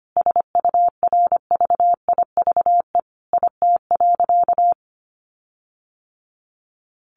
Morse code, heard as a keyed tone, about 25 wpm, 700 Hz. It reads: SURVIVE IT.
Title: Sentences from Top 1300 Words 25wpm